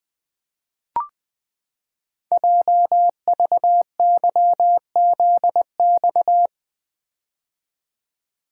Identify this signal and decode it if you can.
Morse code, heard as a keyed tone, about 20 wpm, 700 Hz. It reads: JVYZX
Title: Code Groups - 5 Letters - 20wpm